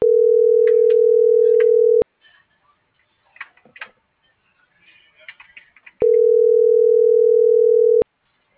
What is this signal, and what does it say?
Ringback tone (North America), 440 + 480 Hz.